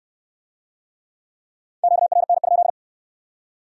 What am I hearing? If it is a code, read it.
Morse code, heard as a keyed tone, about 61 wpm, 700 Hz. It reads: QRN?